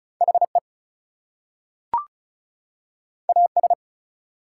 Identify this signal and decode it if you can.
Morse code, heard as a keyed tone, about 36 wpm, 700 Hz. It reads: HE AS
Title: Two Letter Words - 35wpm